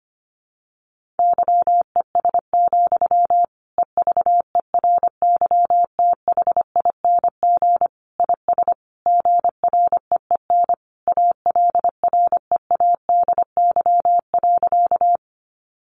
Morse code, heard as a keyed tone, about 25 wpm, 700 Hz. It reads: YES, EVERYTHING IS GREEN ALREADY.
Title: Sentences from Top 700 Words 25wpm